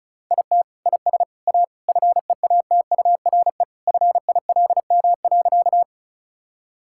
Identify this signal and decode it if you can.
Morse code, heard as a keyed tone, about 34 wpm, 700 Hz. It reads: IT IS A FEATURE FILM.
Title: Sentences from Top 1000 Words - Review Part 5 - 35wpm